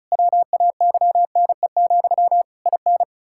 Morse code, heard as a keyed tone, about 36 wpm, 700 Hz. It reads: WAYNE, IN
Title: Practice QSO - Exchange 1 - Beginning - 35wpm